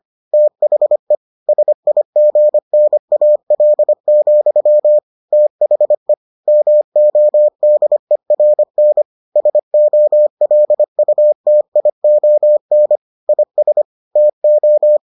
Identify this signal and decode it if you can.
Morse code, heard as a keyed tone, about 25 wpm, 600 Hz. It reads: THE SIGNAL, THE MODERN SOLUTION IS TO